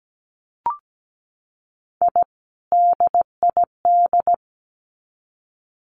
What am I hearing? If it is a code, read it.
Morse code, heard as a keyed tone, about 17 wpm, 700 Hz. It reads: I DID
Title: Sets of 2 Words 17wpm